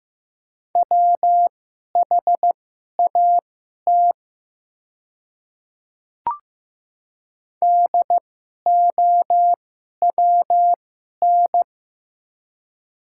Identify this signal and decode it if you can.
Morse code, heard as a keyed tone, about 15 wpm, spaced as for 11 wpm, 690 Hz. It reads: WHAT DOWN
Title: 2x Character Spacing - Top 300 Words - 15wpm